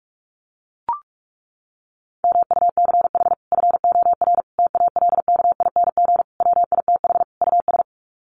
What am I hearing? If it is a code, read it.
Morse code, heard as a keyed tone, about 44 wpm, 700 Hz. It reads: MUCH FOR TALKING WITH US
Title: Sets of 5 Words Encore 45wpm